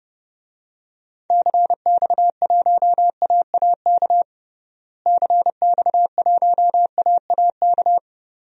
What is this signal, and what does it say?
Morse code, heard as a keyed tone, about 31 wpm, 700 Hz. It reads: CX1AAK CX1AAK
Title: DXCC-100 Call Signs - 6 Characters - Repeated 2x - 30wpm